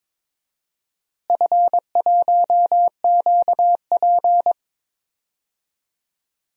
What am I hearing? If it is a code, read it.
Morse code, heard as a keyed tone, about 21 wpm, 700 Hz. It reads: F1QP
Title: DXCC-100 Call Signs - 4 Characters - Repeated 2x - 22wpm